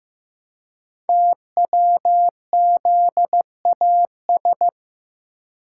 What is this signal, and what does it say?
Morse code, heard as a keyed tone, about 15 wpm, 700 Hz. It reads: TWZAS
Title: Code Groups - 5 Letters - 15wpm